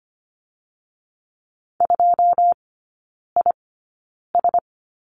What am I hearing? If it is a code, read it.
Morse code, heard as a keyed tone, about 25 wpm, spaced as for 9 wpm, 700 Hz. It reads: 2IS
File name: course / 25:10wpm / 093 Lesson 16e - Letter c - Three Characters 25wpm